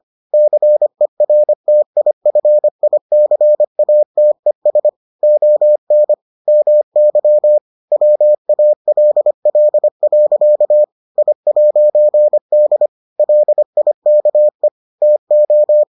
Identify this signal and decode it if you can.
Morse code, heard as a keyed tone, about 25 wpm, 600 Hz. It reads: CERTIFICATES ON MY WALL. I'D LIKE TO